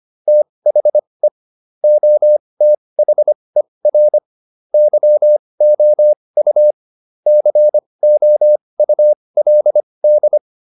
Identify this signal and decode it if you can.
Morse code, heard as a keyed tone, about 25 wpm, spaced as for 20 wpm, 600 Hz. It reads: THE OTHER YOU COULD
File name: SS315 20